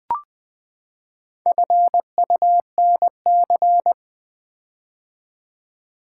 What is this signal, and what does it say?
Morse code, heard as a keyed tone, about 20 wpm, 700 Hz. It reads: FUNC
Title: Build A Word - 4 Letters At A Time - 8-Letter Words - 20wpm